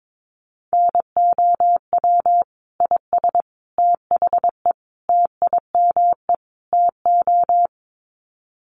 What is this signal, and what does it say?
Morse code, heard as a keyed tone, about 22 wpm, 700 Hz. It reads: NOW IS THE TIME TO
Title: Sets of 5 Words Encore2 22wpm